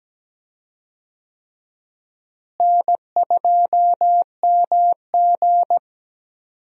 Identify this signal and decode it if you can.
Morse code, heard as a keyed tone, about 17 wpm, 700 Hz. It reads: N2MG
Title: USA Call Signs - 4 Characters - Repeated 3x - 17wpm